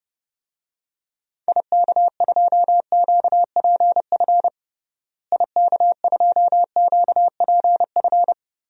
Morse code, heard as a keyed tone, about 30 wpm, 700 Hz. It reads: IK2QPF IK2QPF
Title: DXCC-100 Call Signs - 6 Characters - Repeated 2x - 30wpm